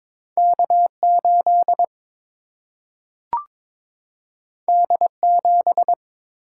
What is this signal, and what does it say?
Morse code, heard as a keyed tone, about 22 wpm, 700 Hz. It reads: K 8 D 7
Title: Single-Letter Single-Number 22wpm